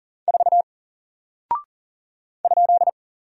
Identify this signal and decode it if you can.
Morse code, heard as a keyed tone, about 40 wpm, 700 Hz. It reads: 4 ?
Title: Single Character (Letter:Number:Punctuation:Prosign) 40wpm